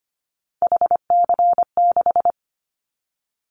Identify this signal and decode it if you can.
Morse code, heard as a keyed tone, about 25 wpm, 700 Hz. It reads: HC6
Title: Callsign Prefixes for Top 100 DXCC - 25wpm